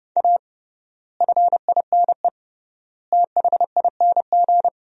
Morse code, heard as a keyed tone, about 30 wpm, 700 Hz. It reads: A FINE THING
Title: 3x Word-Spacing - Sentences from Top 500 Words - 30wpm